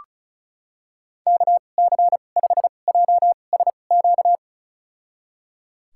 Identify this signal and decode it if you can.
Morse code, heard as a keyed tone, about 36 wpm, spaced as for 25 wpm, 700 Hz. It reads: KC5JSQ
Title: USA Call Signs - 6 Characters - 2x Character Spacing - 35wpm